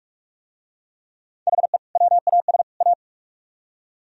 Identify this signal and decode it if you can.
Morse code, heard as a keyed tone, about 44 wpm, 700 Hz. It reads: HE WAS A